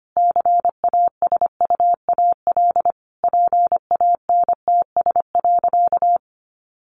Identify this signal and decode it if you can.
Morse code, heard as a keyed tone, about 25 wpm, 700 Hz. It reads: CASUAL PANTS.
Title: Additional Easy Phrases 25wpm